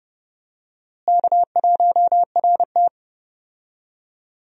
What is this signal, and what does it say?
Morse code, heard as a keyed tone, about 31 wpm, 700 Hz. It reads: K1RT
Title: USA Call Signs - 4 Characters - Repeated 3x - 30wpm